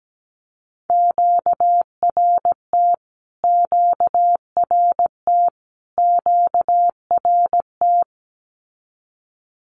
Morse code, heard as a keyed tone, about 17 wpm, 700 Hz. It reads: QRT QRT QRT
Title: Q-Codes Repeated 3x 17wpm